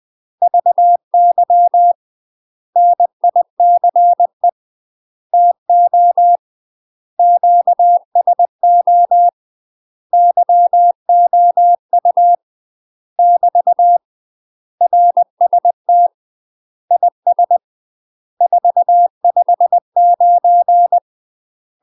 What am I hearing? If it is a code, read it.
Morse code, heard as a keyed tone, about 20 wpm, 700 Hz. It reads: VY NICE TO QSO YOU <BT> RST IS 459
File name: UK MASTER QSOs TRAINING / 20WPM / UK MASTER QSOs 0041